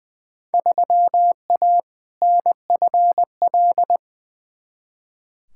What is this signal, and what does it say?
Morse code, heard as a keyed tone, about 21 wpm, 700 Hz. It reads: 3A NFL
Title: Field Day Exchanges - 20wpm